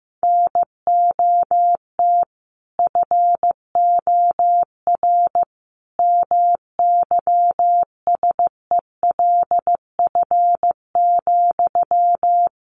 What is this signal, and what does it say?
Morse code, heard as a keyed tone, about 15 wpm, 700 Hz. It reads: NOT FOR MYSELF,
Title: Sentences from Top 1300 Words 15wpm